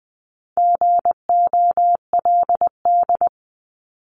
Morse code, heard as a keyed tone, about 20 wpm, 700 Hz. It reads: GOLD
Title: Top 500 Words 20wpm